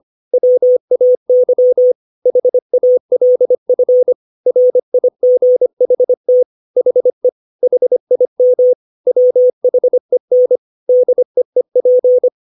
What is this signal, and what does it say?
Morse code, heard as a keyed tone, about 25 wpm, 500 Hz. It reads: WAY HALF RIGHT HE HIM WHEN DEEP
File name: 632 words / 632 Words 25WPM